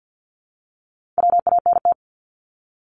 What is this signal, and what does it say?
Morse code, heard as a keyed tone, about 48 wpm, 700 Hz. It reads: WANT
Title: ED Suffix - Top 100 Words with Base Words 50wpm